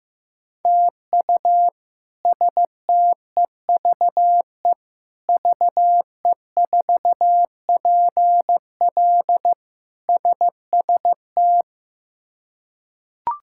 Morse code, heard as a keyed tone, about 16 wpm, 700 Hz. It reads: TU STEVE VE4PL SST
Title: K1USN SST QSO Practice - 15wpm